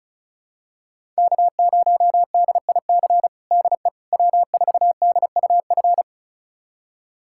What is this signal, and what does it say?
Morse code, heard as a keyed tone, about 36 wpm, 700 Hz. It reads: K0DIC DE W4DUF
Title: USA Callsign DE Callsign - 5 Characters - 35wpm